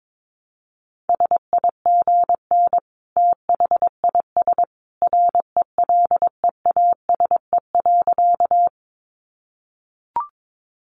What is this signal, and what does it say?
Morse code, heard as a keyed tone, about 22 wpm, 700 Hz. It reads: SIGN THIS RELEASE.
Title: A Sound Of Thunder 22wpm